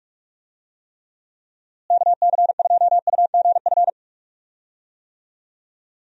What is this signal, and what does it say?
Morse code, heard as a keyed tone, about 41 wpm, 700 Hz. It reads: KC2UGF
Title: USA Call Signs - 6 Characters - Repeated 2x - 45wpm